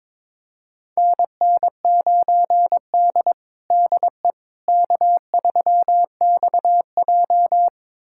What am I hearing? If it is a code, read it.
Morse code, heard as a keyed tone, about 22 wpm, 700 Hz. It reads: NN9D DE K3XJ
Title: USA Callsign DE Callsign - 4 Characters - 22wpm